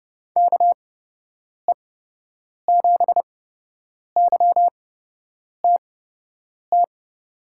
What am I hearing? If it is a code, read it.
Morse code, heard as a keyed tone, about 30 wpm, spaced as for 8 wpm, 700 Hz. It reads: KE7YTT
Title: USA Call Signs - 6 Characters - 8x Character Spacing - 30wpm